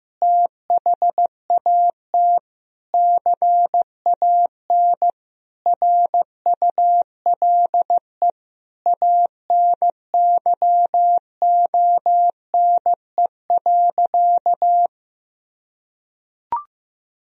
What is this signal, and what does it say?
Morse code, heard as a keyed tone, about 15 wpm, 700 Hz. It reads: THAT CAN RULE ANYONE.
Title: Sentences from Top 1000 Words - Review Part 2 - 15wpm